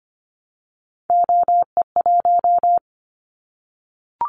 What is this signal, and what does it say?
Morse code, heard as a keyed tone, about 25 wpm, 700 Hz. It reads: OE1
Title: Callsign Prefixes for Top 50 DXCC - 25wpm